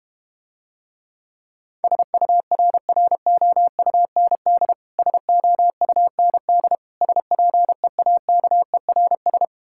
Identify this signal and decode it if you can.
Morse code, heard as a keyed tone, about 31 wpm, 700 Hz. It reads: SURROUND SOUND SPEAKERS